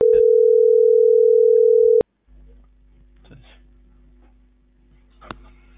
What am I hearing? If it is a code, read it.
Ringback tone (North America), 440 + 480 Hz.